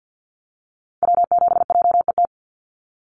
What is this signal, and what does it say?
Morse code, heard as a keyed tone, about 50 wpm, 700 Hz. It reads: W7JET
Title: NA SOTA Activator Callsigns 50wpm